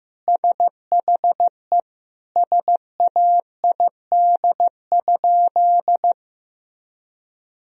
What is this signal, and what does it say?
Morse code, heard as a keyed tone, about 15 wpm, 700 Hz. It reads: SHE SAID?
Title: Sentences from Top 1000 Words - Review Part 1 - 15wpm